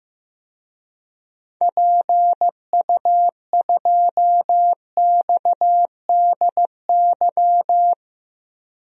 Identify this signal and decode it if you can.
Morse code, heard as a keyed tone, about 15 wpm, 700 Hz. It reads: PU2XDY
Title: DXCC-100 Call Signs - 6 Characters - Repeated 2x - 15wpm